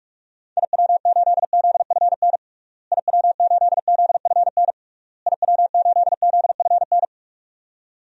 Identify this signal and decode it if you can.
Morse code, heard as a keyed tone, about 46 wpm, 700 Hz. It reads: IW8ZFN IW8ZFN IW8ZFN
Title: DXCC-100 Call Signs - 6 Characters - Repeated 3x - 45wpm